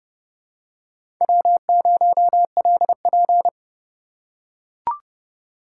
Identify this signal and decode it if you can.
Morse code, heard as a keyed tone, about 32 wpm, 700 Hz. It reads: W0LP
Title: USA Call Signs - 4 Characters - 3x Character Spacing - 30wpm